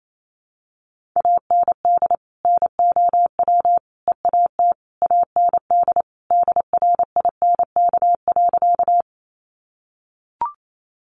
Morse code, heard as a keyed tone, about 28 wpm, 700 Hz. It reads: AND NOW EAT AND DRINK.
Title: Sentences from Top 1500 Words 28wpm